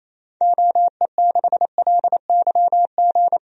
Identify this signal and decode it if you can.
Morse code, heard as a keyed tone, about 28 wpm, 700 Hz. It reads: OE6LYG
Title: DXCC-100 Call Signs - 6 Characters - Repeated 2x - 28wpm